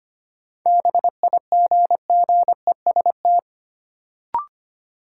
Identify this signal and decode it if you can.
Morse code, heard as a keyed tone, about 25 wpm, 700 Hz. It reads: BIGGEST
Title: Sets of 4 Words - New Word Drill - 25wpm